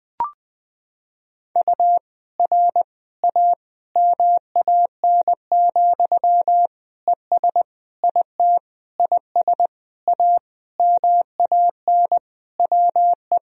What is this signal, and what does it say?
Morse code, heard as a keyed tone, about 20 wpm, 700 Hz. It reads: U R A MAN, ES IT IS A MAN WE
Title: Sentences from Top 200 Words and Abbreviations 20wpm